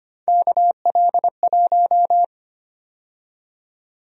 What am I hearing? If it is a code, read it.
Morse code, heard as a keyed tone, about 25 wpm, 700 Hz. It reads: KL1
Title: USA Call Sign Prefixes - 25wpm